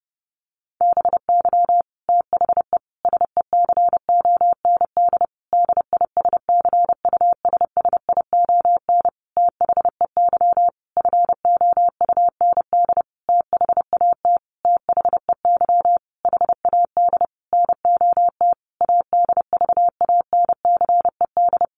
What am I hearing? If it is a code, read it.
Morse code, heard as a keyed tone, about 30 wpm, 700 Hz. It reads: BY THE SECOND DISCUSSION THEY FOUND THAT THEY HAD NOT ADVANCED
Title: Sentences from Top 1200 Words 30wpm